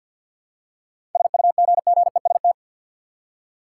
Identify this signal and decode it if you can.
Morse code, heard as a keyed tone, about 52 wpm, 700 Hz. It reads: SUGGEST
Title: Seven Letter Words - 50wpm